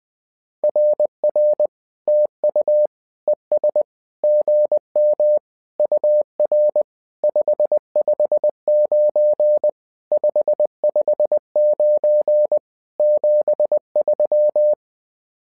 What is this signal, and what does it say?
Morse code, heard as a keyed tone, about 20 wpm, 600 Hz. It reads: RR TU ES GM UR 559 559 73
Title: SOTA QSO Practice 20wpm